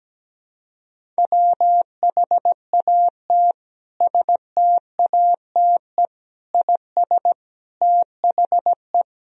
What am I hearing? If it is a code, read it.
Morse code, heard as a keyed tone, about 17 wpm, 700 Hz. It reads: WHAT STATE IS THE
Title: Sentences from Top 1000 Words - Review Part 2 - 17wpm